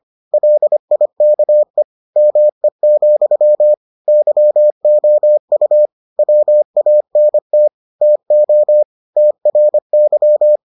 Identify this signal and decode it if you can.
Morse code, heard as a keyed tone, about 25 wpm, 600 Hz. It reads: LIKE ME, YOU WANT TO TRY